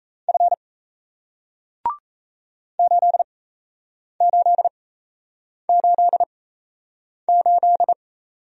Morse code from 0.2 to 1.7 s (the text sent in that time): F